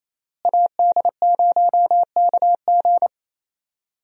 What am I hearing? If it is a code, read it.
Morse code, heard as a keyed tone, about 28 wpm, 700 Hz. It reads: AD0KG
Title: USA Call Signs - 5 Characters - Repeated 3x - 28wpm